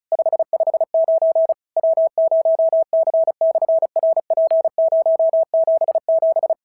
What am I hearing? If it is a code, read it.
Morse code, heard as a keyed tone, about 34 wpm, 650 Hz. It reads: 559 W0C/RP077
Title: SOTA QSO Practice 35wpm